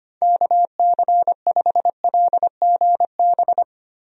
Morse code, heard as a keyed tone, about 25 wpm, 700 Hz. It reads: KC5LGB
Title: USA Call Signs - 6 Characters - Repeated 3x - 25wpm